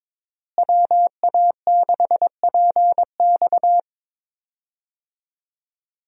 Morse code, heard as a keyed tone, about 22 wpm, 700 Hz. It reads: WA6PX
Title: USA Call Signs - 5 Characters - Repeated 3x - 22wpm